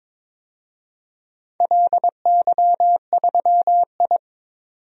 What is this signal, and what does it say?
Morse code, heard as a keyed tone, about 21 wpm, 700 Hz. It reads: LY3I
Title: DXCC-100 Call Signs - 4 Characters - 22wpm